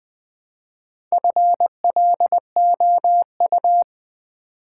Morse code, heard as a keyed tone, about 20 wpm, 700 Hz. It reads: FLOU
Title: Build A Word - 2 Letters At A Time - 8-Letter Words - 20wpm